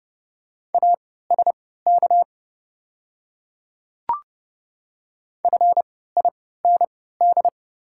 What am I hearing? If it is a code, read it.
Morse code, heard as a keyed tone, about 29 wpm, spaced as for 17 wpm, 690 Hz. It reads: ASK FIND
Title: 3x Character Spacing - Top 200 Words - 30wpm